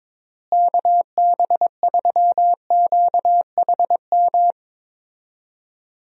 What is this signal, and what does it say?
Morse code, heard as a keyed tone, about 22 wpm, 700 Hz. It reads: KB3QHM
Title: USA Call Signs - 6 Characters - Repeated 3x - 22wpm